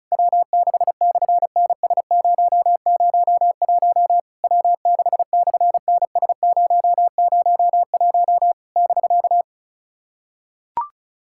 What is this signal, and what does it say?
Morse code, heard as a keyed tone, about 33 wpm, 700 Hz. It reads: W6/NS001 W6/NS001 <BK>
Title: SOTA QSO Practice 35wpm